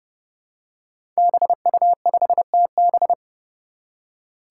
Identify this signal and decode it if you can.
Morse code, heard as a keyed tone, about 30 wpm, 700 Hz. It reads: BU5TB
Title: Five Challenging Letter-Number Combinations 30wpm